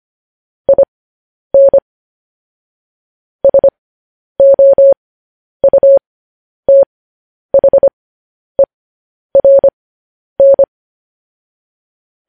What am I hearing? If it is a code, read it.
Morse code, heard as a keyed tone, about 25 wpm, spaced as for 10 wpm, 550 Hz. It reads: IN SOUTHERN
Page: news headlines in morse code at 10 wpm